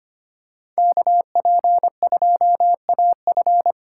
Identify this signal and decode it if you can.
Morse code, heard as a keyed tone, about 25 wpm, 700 Hz. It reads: KP2AF
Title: USA Call Signs - 5 Characters - 8x Character Spacing - 25wpm